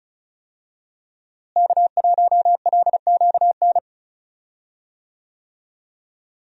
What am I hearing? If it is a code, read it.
Morse code, heard as a keyed tone, about 34 wpm, 700 Hz. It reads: K1LQN
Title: USA Call Signs - 5 Characters - Repeated 2x - 35wpm